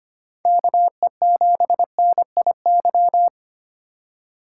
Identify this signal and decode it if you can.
Morse code, heard as a keyed tone, about 26 wpm, 700 Hz. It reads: KE7NIY
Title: Speed Racing - USA 2x3 Call Signs 25wpm